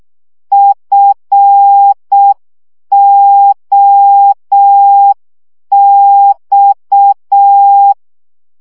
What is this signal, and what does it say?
Morse code, heard as a keyed tone, about 6 wpm, 800 Hz. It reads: FOX
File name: CW 5 baud